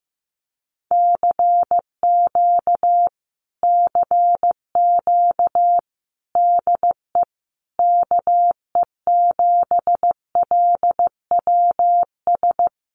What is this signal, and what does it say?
Morse code, heard as a keyed tone, about 15 wpm, 700 Hz. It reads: CQ CQ DE KE7LWS
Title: Common QSO Phrases Encore 15wpm